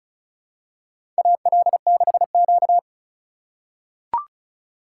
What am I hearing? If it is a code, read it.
Morse code, heard as a keyed tone, about 36 wpm, 700 Hz. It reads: AL6Q
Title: USA Call Signs - 4 Characters - 4x Character Spacing - 35wpm